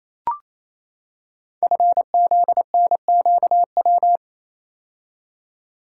Morse code, heard as a keyed tone, about 27 wpm, 700 Hz. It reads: FZNQW
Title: Code Groups - 5 Letters - 28wpm